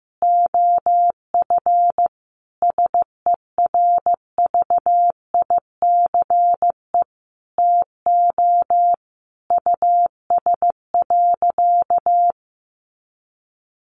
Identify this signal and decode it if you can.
Morse code, heard as a keyed tone, about 15 wpm, 700 Hz. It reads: OF SERVICE TO US.
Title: Sentences from Top 1200 Words 15wpm